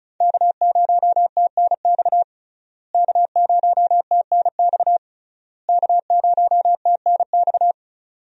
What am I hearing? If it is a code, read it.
Morse code, heard as a keyed tone, about 36 wpm, 700 Hz. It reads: K0TNX K0TNX K0TNX
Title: USA Call Signs - 5 Characters - Repeated 3x - 35wpm